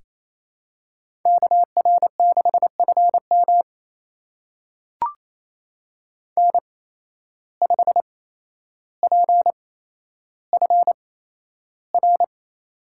Morse code, heard as a keyed tone, about 28 wpm, spaced as for 12 wpm, 700 Hz. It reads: KR6FM N5PFR
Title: USA Call Signs - 5 Characters - 8x Character Spacing - 28wpm